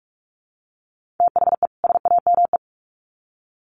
Morse code, heard as a keyed tone, about 44 wpm, 700 Hz. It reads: THE SAME
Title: Sets of 2 Words 45wpm